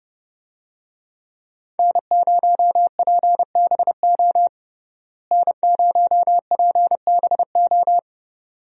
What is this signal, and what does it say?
Morse code, heard as a keyed tone, about 30 wpm, 700 Hz. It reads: N0PBO N0PBO
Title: USA Call Signs - 5 Characters - Repeated 2x - 30wpm